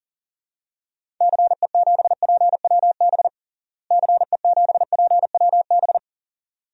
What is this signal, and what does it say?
Morse code, heard as a keyed tone, about 40 wpm, 700 Hz. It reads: CE7PWB CE7PWB
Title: DXCC-100 Call Signs - 6 Characters - Repeated 2x - 40wpm